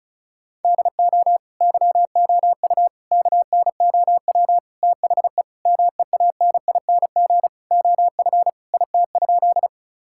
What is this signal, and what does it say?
Morse code, heard as a keyed tone, about 36 wpm, 700 Hz. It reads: DO YOU KNOW THE MEANING OF IT?
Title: Sentences from Top 1000 Words - Review Part 6 - 35wpm